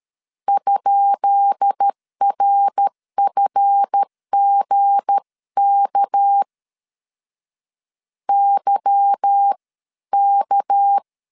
Morse code, heard as a keyed tone, about 13 wpm, spaced as for 11 wpm, 800 Hz. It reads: ?RFGK YK
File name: LEZIONI / 10 SIGNS / 10B INT 12wpm 2414